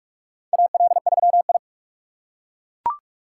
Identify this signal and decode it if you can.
Morse code, heard as a keyed tone, about 42 wpm, 700 Hz. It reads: AL3I
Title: USA Call Signs - 4 Characters - 6x Character Spacing - 45wpm